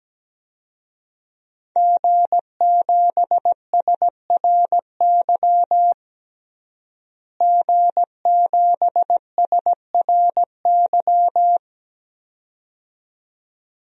Morse code, heard as a keyed tone, about 17 wpm, 700 Hz. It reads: G7SRY G7SRY
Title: DXCC-100 Call Signs - 5 Characters - Repeated 2x - 17wpm